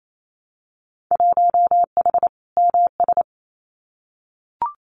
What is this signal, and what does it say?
Morse code, heard as a keyed tone, about 28 wpm, 700 Hz. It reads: 1H MS
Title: Winter Field Day Exchanges 28wpm